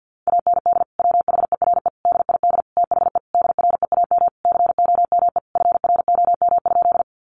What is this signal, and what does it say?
Morse code, heard as a keyed tone, about 50 wpm, 700 Hz. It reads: AND WHERE DID THE DREAM COME FROM?
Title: Sentences from Top 1500 Words 50wpm